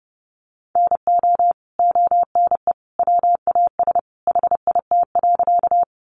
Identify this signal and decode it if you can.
Morse code, heard as a keyed tone, about 30 wpm, 700 Hz. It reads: NO ONE WAS HIT.
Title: Sentences from Top 1200 Words 30wpm